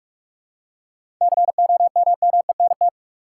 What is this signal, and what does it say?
Morse code, heard as a keyed tone, about 46 wpm, 700 Hz. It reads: COMMENT
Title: Seven Letter Words - 45wpm